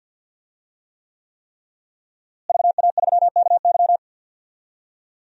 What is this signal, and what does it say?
Morse code, heard as a keyed tone, about 48 wpm, 700 Hz. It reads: VA3KY
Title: DXCC-100 Call Signs - 5 Characters - Repeated 3x - 50wpm